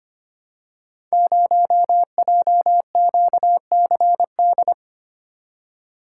Morse code, heard as a keyed tone, about 25 wpm, 700 Hz. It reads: 0JQCD
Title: Code Groups - 5 Letter:Numbers - 25wpm